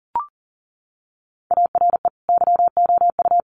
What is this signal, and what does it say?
Morse code, heard as a keyed tone, about 40 wpm, 700 Hz. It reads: ARE YOU
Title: Sets of 2 Words 40wpm